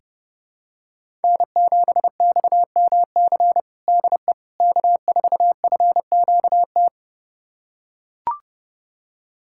Morse code, heard as a keyed tone, about 30 wpm, 700 Hz. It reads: N7XMC DE K4FQT
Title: USA Callsign DE Callsign 30wpm